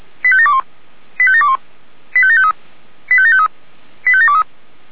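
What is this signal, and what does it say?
Five-tone selective call (ZVEI), heard as quick runs of five tones, tones of about 70 ms. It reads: 85721 85721 85753 85753 85723
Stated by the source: ZVEI five-tone selective call 85724 received by SDR.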